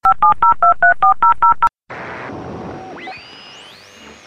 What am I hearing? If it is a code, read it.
DTMF, heard as telephone touch tones, tones of about 120 ms. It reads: 570234000
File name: SaturdayMorningSerial DialingIn free103point9 WGXC